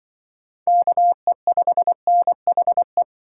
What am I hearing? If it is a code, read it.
Morse code, heard as a keyed tone, about 25 wpm, 700 Hz. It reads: KE5NHE
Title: Speed Racing - USA 2x3 Call Signs 20wpm